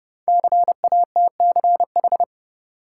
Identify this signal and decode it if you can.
Morse code, heard as a keyed tone, about 30 wpm, 700 Hz. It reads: CATCH
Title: Sentences from Top 1500 Words - New Word Drill - 30wpm